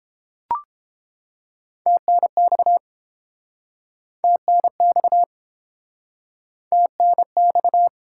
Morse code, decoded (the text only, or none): TNX TNX TNX